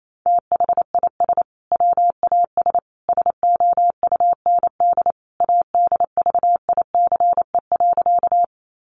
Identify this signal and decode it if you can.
Morse code, heard as a keyed tone, about 28 wpm, 700 Hz. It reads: THIS WAS SOUND ADVICE.
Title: Sentences from Top 1100 Words 28wpm